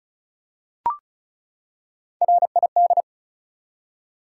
Morse code, heard as a keyed tone, about 36 wpm, 700 Hz. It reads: RID
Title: Three Letter Words - 35wpm